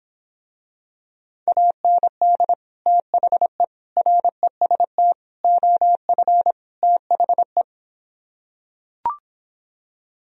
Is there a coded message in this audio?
Yes: Morse code from 1.5 to 7.6 s, heard as a keyed tone, about 26 wpm, 700 Hz. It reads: AND THE REST OF THE